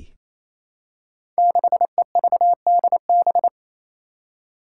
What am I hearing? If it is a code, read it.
Morse code, heard as a keyed tone, about 28 wpm, 700 Hz. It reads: 6EVDB
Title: Five Challenging Letter-Number Combinations 28wpm